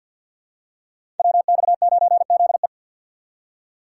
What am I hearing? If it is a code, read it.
Morse code, heard as a keyed tone, about 51 wpm, 700 Hz. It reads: WX9ZE
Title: Code Groups - 5 Letter:Numbers - 50wpm